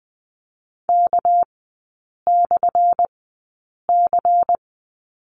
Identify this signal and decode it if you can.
Morse code, heard as a keyed tone, about 20 wpm, spaced as for 8 wpm, 700 Hz. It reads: K/C
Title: 198 Lesson 32e - Letter k - Three Characters 20wpm